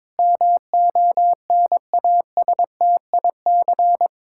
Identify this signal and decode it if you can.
Morse code, heard as a keyed tone, about 22 wpm, 700 Hz. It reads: MONASTIC
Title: Pangram Challenge v2 - 22wpm